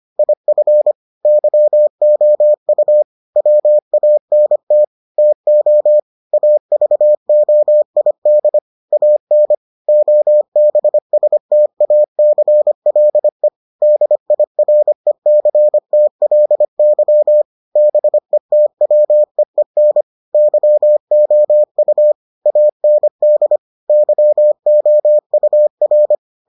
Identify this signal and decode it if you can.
Morse code, heard as a keyed tone, about 25 wpm, 600 Hz. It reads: IF YOU WANT TO AVOID AN OBSTACLE DIRECTLY BETWEEN YOU AND YOUR